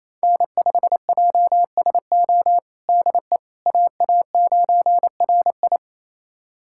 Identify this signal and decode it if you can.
Morse code, heard as a keyed tone, about 27 wpm, 700 Hz. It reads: N5JSO DE AA9RI
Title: USA Callsign DE Callsign - 5 Characters - 28wpm